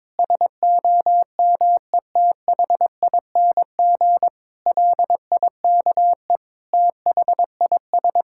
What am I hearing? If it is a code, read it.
Morse code, heard as a keyed tone, about 22 wpm, 700 Hz. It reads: SOMETHING LIKE THIS
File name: practice / 22wpm / Sentences with 8-Word Phrases - 22wpm